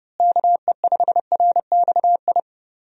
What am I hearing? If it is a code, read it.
Morse code, heard as a keyed tone, about 30 wpm, 700 Hz. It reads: KE5RXI
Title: USA Call Signs - 6 Characters - Repeated 3x - 30wpm